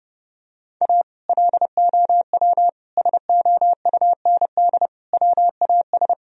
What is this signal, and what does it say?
Morse code, heard as a keyed tone, about 32 wpm, 700 Hz. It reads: A LOW SOUND WAS
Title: Sentences from Top 1000 Words - Review Part 1 - 30wpm